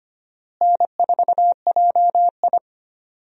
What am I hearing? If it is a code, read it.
Morse code, heard as a keyed tone, about 25 wpm, 700 Hz. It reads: N4JI
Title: USA Call Signs - 4 Characters - Repeated 2x - 25wpm